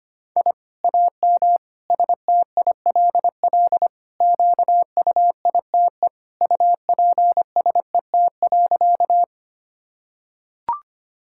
Morse code, heard as a keyed tone, about 25 wpm, 700 Hz. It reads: I AM STILL QUITE UPSET.
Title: Sentences from Top 1000 Words - Review Part 5 - 25wpm